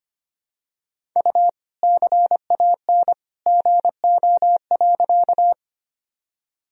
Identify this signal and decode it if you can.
Morse code, heard as a keyed tone, about 25 wpm, 700 Hz. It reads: U CAN GO.
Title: Sentences from Top 100 Words and Abbreviations 25wpm